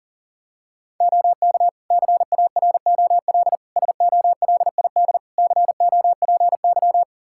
Morse code, heard as a keyed tone, about 40 wpm, 700 Hz. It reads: OK CAROL SOLID COPY
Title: K1USN SST QSO Practice 40wpm